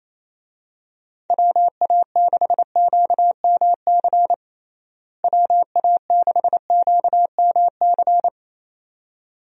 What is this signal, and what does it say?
Morse code, heard as a keyed tone, about 28 wpm, 700 Hz. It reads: WA6QMC WA6QMC
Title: USA Call Signs - 6 Characters - Repeated 2x - 28wpm